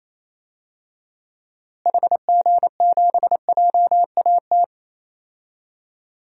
Morse code, heard as a keyed tone, about 27 wpm, 700 Hz. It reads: HG7JAT
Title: DXCC-100 Call Signs - 6 Characters - Repeated 2x - 28wpm